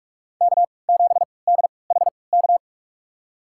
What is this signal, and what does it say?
Morse code, heard as a keyed tone, about 46 wpm, spaced as for 26 wpm, 700 Hz. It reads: K7DHK
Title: USA Call Signs - 5 Characters - 3x Character Spacing - 45wpm